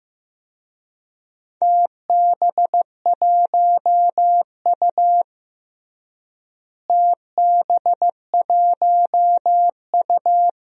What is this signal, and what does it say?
Morse code, heard as a keyed tone, about 15 wpm, 700 Hz. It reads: TB1U TB1U
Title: DXCC-100 Call Signs - 4 Characters - Repeated 2x - 15wpm